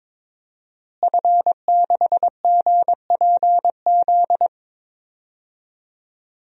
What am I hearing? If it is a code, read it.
Morse code, heard as a keyed tone, about 22 wpm, 700 Hz. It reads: F6GPZ
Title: DXCC-100 Call Signs - 5 Characters - Repeated 3x - 22wpm